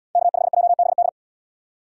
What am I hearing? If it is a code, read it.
Morse code, heard as a keyed tone, about 74 wpm, 700 Hz. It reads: K5JBD